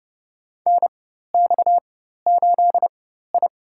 Morse code, heard as a keyed tone, about 30 wpm, spaced as for 14 wpm, 700 Hz. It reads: NX8I
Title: USA Call Signs - 4 Characters - 4x Character Spacing - 30wpm